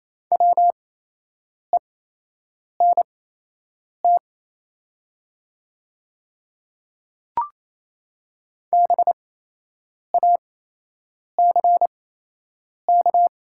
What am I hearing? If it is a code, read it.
Morse code, heard as a keyed tone, about 27 wpm, spaced as for 8 wpm, 700 Hz. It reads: WENT BACK
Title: 8x Character Spacing - Top 200 Words - 28wpm